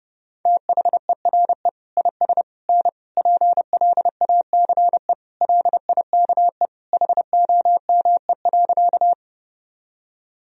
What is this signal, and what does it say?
Morse code, heard as a keyed tone, about 30 wpm, 700 Hz. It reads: THERE IS N PLACE LIKE HOME.
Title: Sentences from Top 200 Words and Abbreviations 30wpm